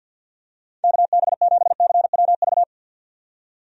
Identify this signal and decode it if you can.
Morse code, heard as a keyed tone, about 52 wpm, 700 Hz. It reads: KB7CWV
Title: USA Call Signs - 6 Characters - 50wpm